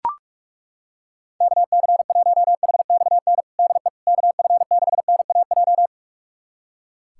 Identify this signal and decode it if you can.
Morse code, heard as a keyed tone, about 42 wpm, 700 Hz. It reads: KC1HXN DE KF6NAJ
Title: USA Callsign DE Callsign - 6 Characters - 45wpm